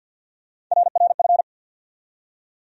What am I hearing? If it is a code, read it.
Morse code, heard as a keyed tone, about 51 wpm, 700 Hz. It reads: RRF
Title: USA Call Sign Postfixes - 3 Letters 50wpm